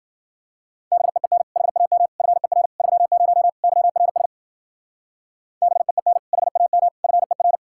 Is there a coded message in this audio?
Yes: Morse code from 0.9 to 7.6 s, heard as a keyed tone, about 52 wpm, 700 Hz. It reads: BEEN HAM FER 30 YRS BEEN HAM FER